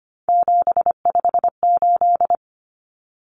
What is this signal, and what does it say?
Morse code, heard as a keyed tone, about 25 wpm, 700 Hz. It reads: 758
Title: Common Cut Numbers - 3-Digit 25wpm